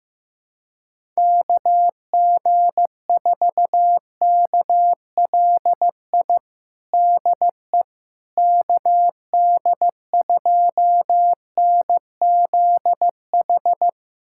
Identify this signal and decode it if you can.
Morse code, heard as a keyed tone, about 16 wpm, 700 Hz. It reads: KG4KLI DE KD2NZH
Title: USA Callsign DE Callsign 15wpm